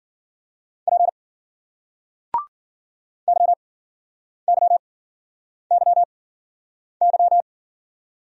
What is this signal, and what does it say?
Morse code, heard as a keyed tone, about 53 wpm, spaced as for 9 wpm, 700 Hz. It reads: P YYYY